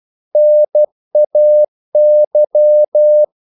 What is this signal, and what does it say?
Morse code, heard as a keyed tone, about 12 wpm, 600 Hz. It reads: NAY
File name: CW / Practice / NationalCapitals12